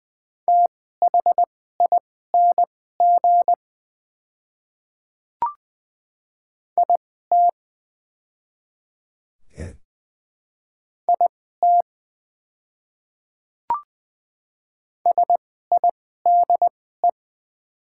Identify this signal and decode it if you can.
Morse code, heard as a keyed tone, about 20 wpm, spaced as for 14 wpm, 700 Hz. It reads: THING IT IT SIDE
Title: 2x Character Spacing - Top 100 Words - 20wpm